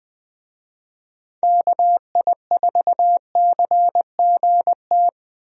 Morse code, heard as a keyed tone, about 20 wpm, 700 Hz. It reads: KI4CGT
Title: USA Call Signs - 6 Characters - Repeated 2x - 20wpm